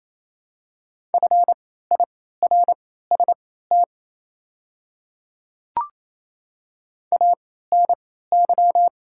Morse code, heard as a keyed tone, about 28 wpm, spaced as for 16 wpm, 700 Hz. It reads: FIRST ANY
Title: 3x Character Spacing - Top 200 Words - 28wpm